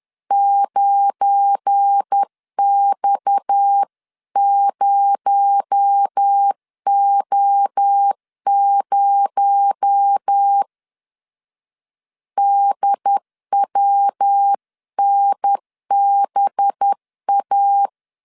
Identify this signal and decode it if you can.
Morse code, heard as a keyed tone, about 10 wpm, 800 Hz. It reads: 9X0O0 DWNBA
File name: CW / DET / 1 9 1500C S A 10wpm 4511